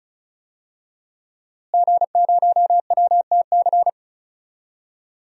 Morse code, heard as a keyed tone, about 36 wpm, 700 Hz. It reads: G0WTC
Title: DXCC-100 Call Signs - 5 Characters - Repeated 2x - 35wpm